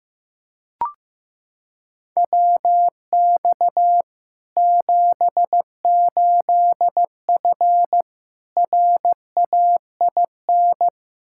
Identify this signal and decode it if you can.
Morse code, heard as a keyed tone, about 15 wpm, 700 Hz. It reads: WX 78F RAIN
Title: K1USN SST QSO Practice 15wpm